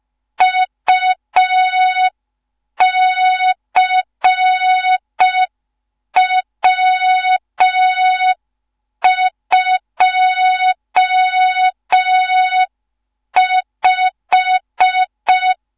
Morse code, heard as a keyed tone, about 5 wpm, 3700 Hz. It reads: UCW25
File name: Morze-HA2MN / hang / 15 GYAKORLAT